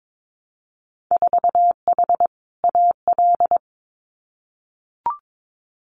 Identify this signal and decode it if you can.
Morse code, heard as a keyed tone, about 22 wpm, 700 Hz. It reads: 4H AL
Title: Winter Field Day Exchanges Encore 22wpm